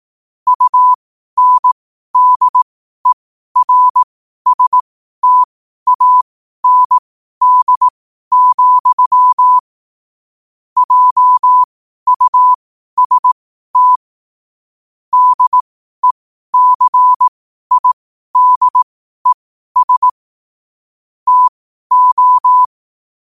Morse code, heard as a keyed tone, about 18 wpm, spaced as for 12 wpm, 1000 Hz. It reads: UNDERSTAND, JUST DECIDES TO
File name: qotdfiles / 12wpm